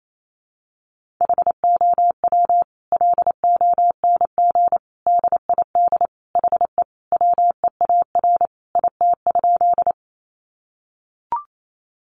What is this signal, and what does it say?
Morse code, heard as a keyed tone, about 28 wpm, 700 Hz. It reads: HOW LONG DID HE WEAR IT?
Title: Sentences from Top 1500 Words 28wpm